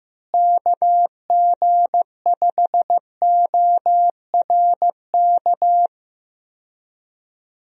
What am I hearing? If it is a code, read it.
Morse code, heard as a keyed tone, about 15 wpm, 700 Hz. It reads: KG5ORK
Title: USA Call Signs - 6 Characters - Repeated 2x - 15wpm